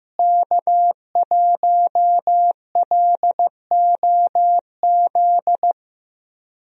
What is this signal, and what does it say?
Morse code, heard as a keyed tone, about 15 wpm, 700 Hz. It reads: K1LOZ
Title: USA Call Signs - 5 Characters - Repeated 3x - 15wpm